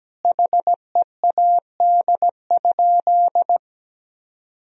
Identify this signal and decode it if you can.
Morse code, heard as a keyed tone, about 17 wpm, 700 Hz. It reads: HEAD?
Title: Sentences from Top 1000 Words - Review Part 3 - 17wpm